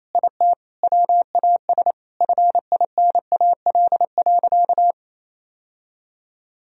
Morse code, heard as a keyed tone, about 28 wpm, 700 Hz. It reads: IT WAS FINAL.
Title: Sentences from Top 1000 Words - Review Part 2 - 28wpm